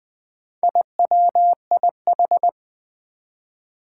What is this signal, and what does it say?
Morse code, heard as a keyed tone, about 20 wpm, 700 Hz. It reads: IWIH
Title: Four Challenging Letter-Number Combinations 20wpm